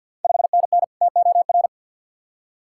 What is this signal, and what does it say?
Morse code, heard as a keyed tone, about 52 wpm, 700 Hz. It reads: 5NN TOR
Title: Ontario QSO Party Exchange 50wpm